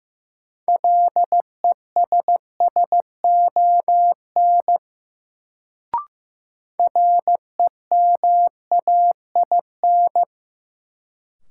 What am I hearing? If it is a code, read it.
Morse code, heard as a keyed tone, about 15 wpm, 700 Hz. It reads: LESSON REMAIN
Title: Sentences from Top 1200 Words - New Word Drill Alt - 15wpm